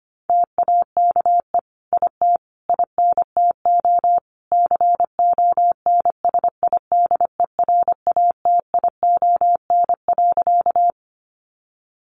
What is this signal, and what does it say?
Morse code, heard as a keyed tone, about 25 wpm, 700 Hz. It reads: TAKE IT INTO CONSIDERATION.
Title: Sentences from Top 1500 Words 25wpm